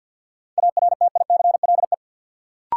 Morse code, heard as a keyed tone, about 53 wpm, 700 Hz. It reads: ARTICLE
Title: Sentences from Top 1000 Words - Review Part 5 - New Word Drill - 50wpm